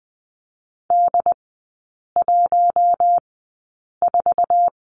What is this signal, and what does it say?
Morse code, heard as a keyed tone, about 20 wpm, spaced as for 8 wpm, 700 Hz. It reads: D14
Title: 074 Lesson 13e - Letter L - Three Characters 20wpm